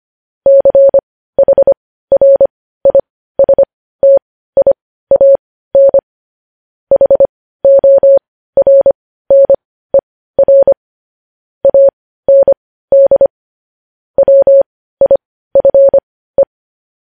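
Morse code, heard as a keyed tone, about 25 wpm, spaced as for 15 wpm, 550 Hz. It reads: CHRISTIAN HORNER AND WIFE
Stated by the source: Morse news headlines practice sent at 15 WPM.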